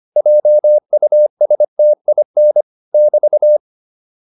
Morse code, heard as a keyed tone, about 25 wpm, 600 Hz. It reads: JUSTIN <BT>
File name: QSO205 25